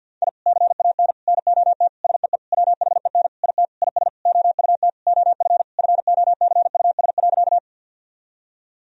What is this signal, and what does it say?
Morse code, heard as a keyed tone, about 50 wpm, 700 Hz. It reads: I CAN NOT SEE WHEN IT IS OUT OF FOCUS.
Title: Sentences from Top 1000 Words - Review Part 4 - 50wpm